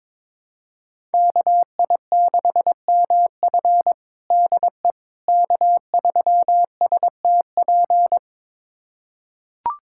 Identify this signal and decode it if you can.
Morse code, heard as a keyed tone, about 22 wpm, 700 Hz. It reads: KI6MF DE K3STP
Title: USA Callsign DE Callsign - 5 Characters - 22wpm